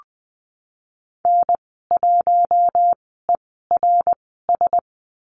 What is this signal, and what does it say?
Morse code, heard as a keyed tone, about 20 wpm, spaced as for 15 wpm, 700 Hz. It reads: N1ERS
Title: 055 Lesson 10g - Letter r - Call Signs 20wpm